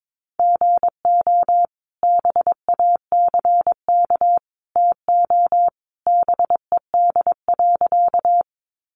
Morse code, heard as a keyed tone, about 22 wpm, 700 Hz. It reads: GO BACK TO BED.
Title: Sentences from Top 500 Words 22wpm